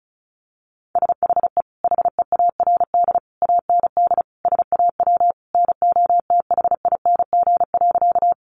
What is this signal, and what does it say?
Morse code, heard as a keyed tone, about 34 wpm, 700 Hz. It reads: SHE HEARD AND SAW NOTHING.
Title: Sentences from Top 500 Words 35wpm